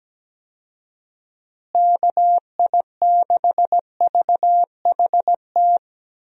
Morse code, heard as a keyed tone, about 17 wpm, 700 Hz. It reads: KI6VHT
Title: USA Call Signs - 6 Characters - Repeated 3x - 17wpm